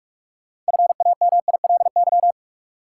Morse code, heard as a keyed tone, about 46 wpm, 700 Hz. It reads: FAMILY
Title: Sentences from Top 400 Words - New Word Drill - 45wpm